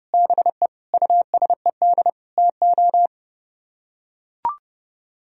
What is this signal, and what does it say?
Morse code, heard as a keyed tone, about 29 wpm, 700 Hz. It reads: BE USED TO